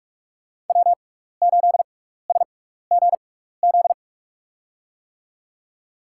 Morse code, heard as a keyed tone, about 42 wpm, spaced as for 15 wpm, 700 Hz. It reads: W8SGZ
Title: USA Call Signs - 5 Characters - 6x Character Spacing - 45wpm